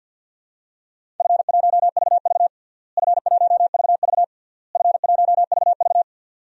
Morse code, heard as a keyed tone, about 48 wpm, 700 Hz. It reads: F1VV F1VV F1VV
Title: DXCC-100 Call Signs - 4 Characters - Repeated 3x - 50wpm